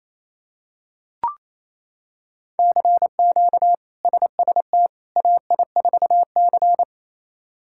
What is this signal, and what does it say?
Morse code, heard as a keyed tone, about 28 wpm, 700 Hz. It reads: CQ SST AI4C
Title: K1USN SST QSO Practice - 28wpm